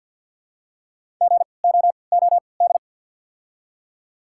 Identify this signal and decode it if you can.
Morse code, heard as a keyed tone, about 49 wpm, 690 Hz. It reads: G O O D
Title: 3x Character Spacing - Top 300 Words - 50wpm